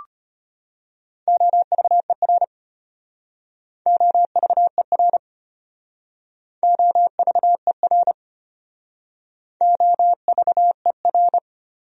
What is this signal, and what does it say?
Morse code, heard as a keyed tone, about 33 wpm, 700 Hz. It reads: OVER OVER OVER OVER